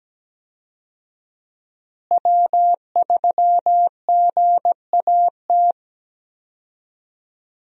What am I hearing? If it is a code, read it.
Morse code, heard as a keyed tone, about 17 wpm, 700 Hz. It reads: W3GAT
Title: USA Call Signs - 5 Characters - Repeated 3x - 17wpm